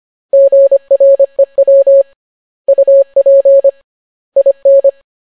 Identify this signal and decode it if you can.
Morse code, heard as a keyed tone, about 25 wpm, 550 Hz. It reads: GREW UP IN